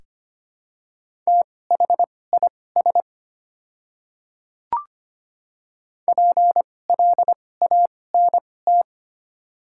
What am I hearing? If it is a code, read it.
Morse code, heard as a keyed tone, about 25 wpm, spaced as for 18 wpm, 700 Hz. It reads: THIS PLANT
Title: 2x Character Spacing - Top 500 Words - 25wpm